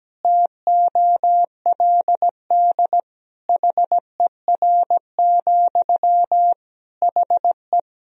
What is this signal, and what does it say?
Morse code, heard as a keyed tone, about 17 wpm, 700 Hz. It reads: TOLD HER, HE
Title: Sentences from Top 1000 Words - Review Part 4 - 17wpm